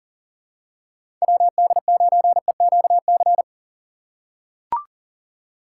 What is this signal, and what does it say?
Morse code, heard as a keyed tone, about 40 wpm, 700 Hz. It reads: WD9EQC
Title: USA Call Signs - 6 Characters - 6x Character Spacing - 40wpm